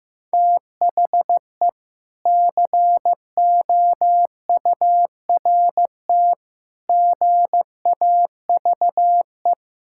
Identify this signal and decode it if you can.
Morse code, heard as a keyed tone, about 14 wpm, 700 Hz. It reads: THE COURT GAVE
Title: Sentences from Top 1600 Words 15wpm